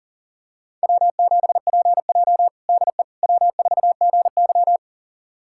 Morse code, heard as a keyed tone, about 40 wpm, 700 Hz. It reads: W7PJ DE W4GY
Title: USA Callsign DE Callsign - 4 Characters - 40wpm